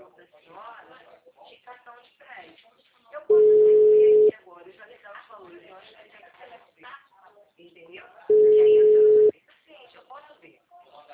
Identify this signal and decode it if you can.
Ringback tone (Europe, 425 Hz), with speech.
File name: out- -2045- - - .21178